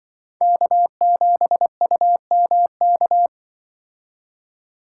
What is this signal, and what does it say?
Morse code, heard as a keyed tone, about 23 wpm, 700 Hz. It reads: K7UMK
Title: Speed Racing - USA 1x3 Call Signs 20wpm